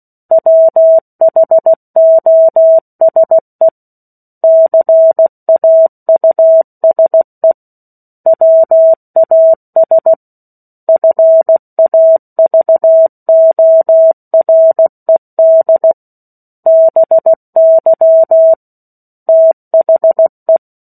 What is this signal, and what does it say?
Morse code, heard as a keyed tone, about 16 wpm, 650 Hz. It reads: WHOSE CAUSE WAS FAVORED BY THE
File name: 03-Chapter-03 16 WPM